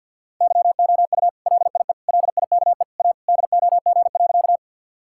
Morse code, heard as a keyed tone, about 50 wpm, 700 Hz. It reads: YOU LIE LIKE A DOG.
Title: Sentences from Top 1600 Words 50wpm